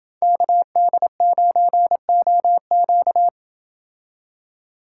Morse code, heard as a keyed tone, about 29 wpm, 700 Hz. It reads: KD9OQ